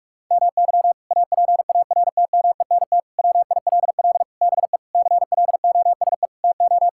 Morse code, heard as a keyed tone, about 46 wpm, 700 Hz. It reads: MY APARTMENT WILL BE CLOSE TO
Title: Sentences from Top 1000 Words - Review Part 6 - 45wpm